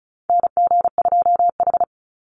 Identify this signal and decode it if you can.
Morse code, heard as a keyed tone, about 36 wpm, 700 Hz. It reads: NG2H
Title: Call Signs - 35wpm